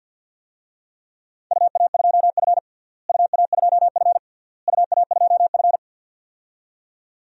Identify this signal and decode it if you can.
Morse code, heard as a keyed tone, about 47 wpm, 700 Hz. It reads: UA2F UA2F UA2F
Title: DXCC-100 Call Signs - 4 Characters - Repeated 3x - 50wpm